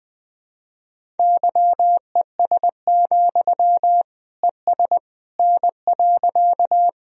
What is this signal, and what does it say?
Morse code, heard as a keyed tone, about 20 wpm, 700 Hz. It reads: YES, ES N.
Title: Sentences from Top 500 Words and Abbreviations 20wpm